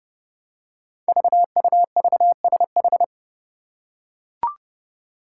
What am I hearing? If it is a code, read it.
Morse code, heard as a keyed tone, about 30 wpm, 700 Hz. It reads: VUVSH
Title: Five Challenging Letter-Number Combinations 30wpm